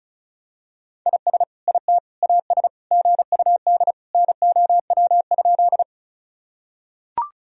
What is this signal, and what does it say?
Morse code, heard as a keyed tone, about 34 wpm, 690 Hz. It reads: IS IT AS GUD NOW?
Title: Sentences from Top 200 Words and Abbreviations 35wpm